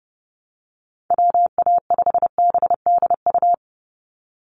Morse code, heard as a keyed tone, about 30 wpm, 700 Hz. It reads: WA5BDU
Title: Call Signs Repeated 3x - 30wpm